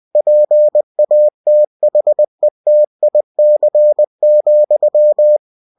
Morse code, heard as a keyed tone, about 20 wpm, 600 Hz. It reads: PATHETIC,
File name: cw-books / winnie-the-pooh / winnie-20wpm-page-0044